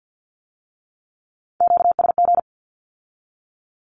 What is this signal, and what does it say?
Morse code, heard as a keyed tone, about 50 wpm, 700 Hz. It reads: QSG
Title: Q-Codes with Spoken Meaning - 50wpm